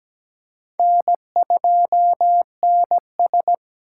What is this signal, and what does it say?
Morse code, heard as a keyed tone, about 17 wpm, 700 Hz. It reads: N2NS
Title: K1USN SST QSO Practice - 17wpm